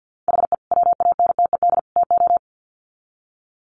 Morse code, heard as a keyed tone, about 50 wpm, 700 Hz. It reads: HE WANTED TO
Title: Sets of 3 Words 50wpm